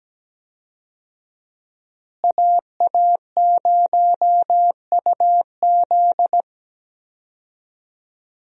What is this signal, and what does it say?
Morse code, heard as a keyed tone, about 17 wpm, 700 Hz. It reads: AA0UZ
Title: USA Call Signs - 5 Characters - Repeated 2x - 17wpm